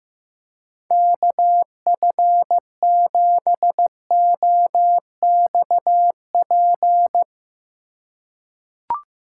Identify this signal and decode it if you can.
Morse code, heard as a keyed tone, about 15 wpm, 700 Hz. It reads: KF7OXP
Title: USA Call Signs - 6 Characters - 3x Character Spacing - 15wpm